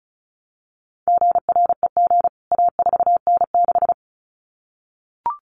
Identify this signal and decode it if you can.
Morse code, heard as a keyed tone, about 34 wpm, 700 Hz. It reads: GREG A4N6
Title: CWT Exchange 35wpm